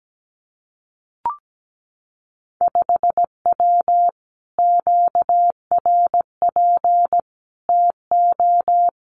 Morse code, heard as a keyed tone, about 17 wpm, 700 Hz. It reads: 5W QRP TO
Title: Common QSO Phrases Encore 17wpm